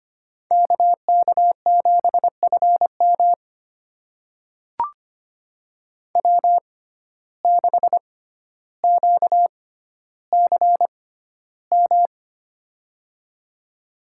Morse code, heard as a keyed tone, about 25 wpm, spaced as for 13 wpm, 700 Hz. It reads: KK7FM W6QCM
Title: USA Call Signs - 5 Characters - 6x Character Spacing - 25wpm